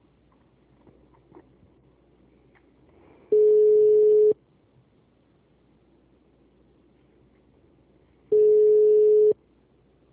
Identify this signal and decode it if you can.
Ringback tone (Europe, 425 Hz).
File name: out- -2006- - - .160